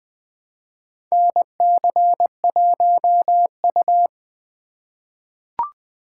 Morse code, heard as a keyed tone, about 20 wpm, 700 Hz. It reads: NC1U
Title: USA Call Signs - 4 Characters - 4x Character Spacing - 20wpm